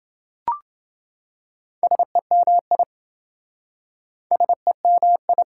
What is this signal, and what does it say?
Morse code, heard as a keyed tone, about 28 wpm, 700 Hz. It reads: SEMI SEMI